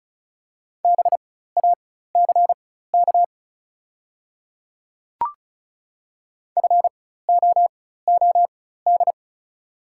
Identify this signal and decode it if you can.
Morse code, heard as a keyed tone, about 35 wpm, spaced as for 16 wpm, 700 Hz. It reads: BACK FOOD
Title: 4x Character Spacing - Top 300 Words - 35wpm